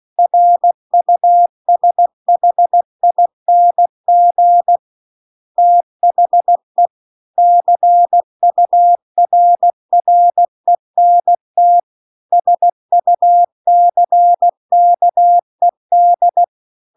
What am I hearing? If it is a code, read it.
Morse code, heard as a keyed tone, about 16 wpm, 700 Hz. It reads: RUSHING THE CURRENT SUCKED
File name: CW-The-Big-Twoheaded-River-2-16WPM